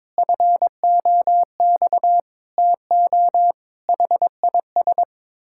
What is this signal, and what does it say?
Morse code, heard as a keyed tone, about 22 wpm, 700 Hz. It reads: FOX TO HIS
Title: Pangram Challenge v2 - 22wpm